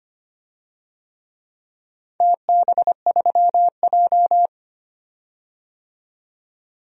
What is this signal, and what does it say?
Morse code, heard as a keyed tone, about 25 wpm, 700 Hz. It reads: TB3J
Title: DXCC-100 Call Signs - 4 Characters - 25wpm